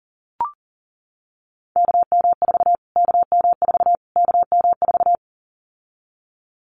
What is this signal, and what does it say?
Morse code, heard as a keyed tone, about 40 wpm, 700 Hz. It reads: KM4 KM4 KM4
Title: Callsign Prefixes for Top 25 DXCC - Repeated 3x - 40wpm